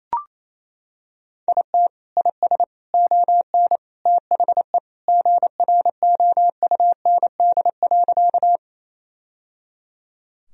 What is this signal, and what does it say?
Morse code, heard as a keyed tone, about 28 wpm, 700 Hz. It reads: IT IS ON THE GROUND.
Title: Sentences from Top 1000 Words - Review Part 2 - 28wpm